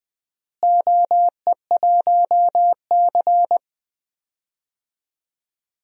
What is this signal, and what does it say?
Morse code, heard as a keyed tone, about 21 wpm, 700 Hz. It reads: OE1C
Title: DXCC-100 Call Signs - 4 Characters - Repeated 2x - 20wpm